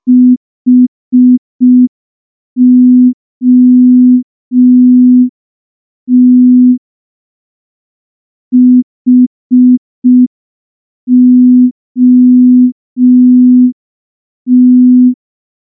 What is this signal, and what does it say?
Morse code, heard as a keyed tone, about 5 wpm, 250 Hz. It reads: HOT HOT
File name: morse / morse-training-05WPM